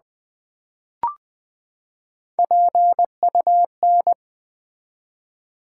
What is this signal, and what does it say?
Morse code, heard as a keyed tone, about 20 wpm, 700 Hz. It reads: PUN
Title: Build A Word - 3 Letters At A Time - 6-Letter Words - 20wpm